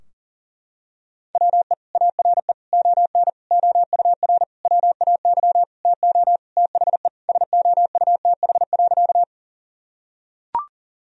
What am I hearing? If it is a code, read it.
Morse code, heard as a keyed tone, about 40 wpm, 700 Hz. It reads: WE ARE ON OUR WAY TO THE SOUTH.
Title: Sentences from Top 1000 Words - Review Part 2 - 40wpm